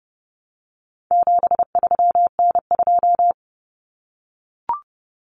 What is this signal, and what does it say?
Morse code, heard as a keyed tone, about 30 wpm, 700 Hz. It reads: 73N2
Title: Common Cut Numbers - 4-Digit 30wpm